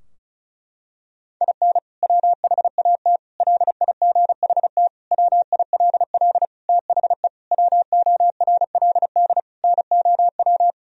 Morse code, heard as a keyed tone, about 36 wpm, 700 Hz. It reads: IN WHAT LIGHT WILL THE WORLD NOW
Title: Sentences from Top 1000 Words - Review Part 2 - 35wpm